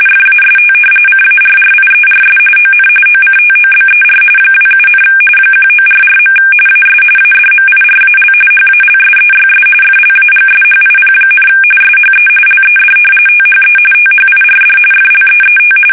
RTTY, about 45.45 baud. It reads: FAST CARRIERS ON OCTOBER 27, 1944. AS ADMIRAL HALSEYS FLAGSHIP, SHE LED THE THIRD FLEET
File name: fsk850 / battleshipnj